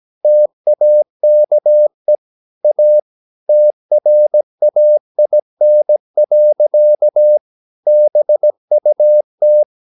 Morse code, heard as a keyed tone, about 17 wpm, 600 Hz. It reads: TAKE A TRAIN. BUT
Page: Romanen Never af Ken Follet på engelsk.